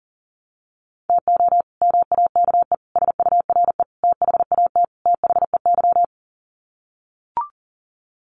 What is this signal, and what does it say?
Morse code, heard as a keyed tone, about 40 wpm, 700 Hz. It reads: TO MAKE SURE THAT THEY
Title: Sets of 5 Words Encore2 40wpm